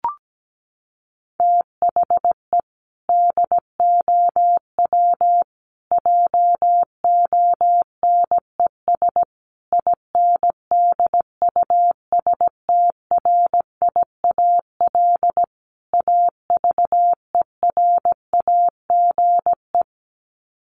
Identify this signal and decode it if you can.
Morse code, heard as a keyed tone, about 17 wpm, 700 Hz. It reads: THE DOW JONES INDUSTRIAL AVERAGE
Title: Sets of 5 Words 17wpm